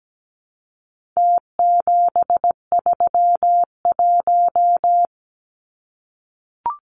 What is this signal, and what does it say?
Morse code, heard as a keyed tone, about 17 wpm, 700 Hz. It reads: T731
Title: Numbers - Contest Style - 4-Digit 17wpm